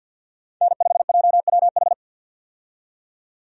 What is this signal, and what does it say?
Morse code, heard as a keyed tone, about 47 wpm, 700 Hz. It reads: N5JWH
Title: USA Call Signs - 5 Characters - 50wpm